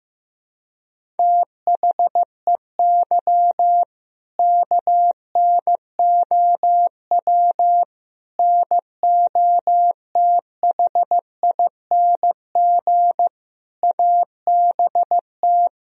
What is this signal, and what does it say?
Morse code, heard as a keyed tone, about 15 wpm, 700 Hz. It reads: THEY KNOW NOTHING ABT
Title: Sentences from Top 500 Words and Abbreviations 15wpm